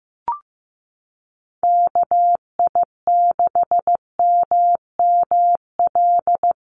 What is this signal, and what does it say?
Morse code, heard as a keyed tone, about 15 wpm, 700 Hz. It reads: KI6MML
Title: USA Call Signs Repeated 3x 15wpm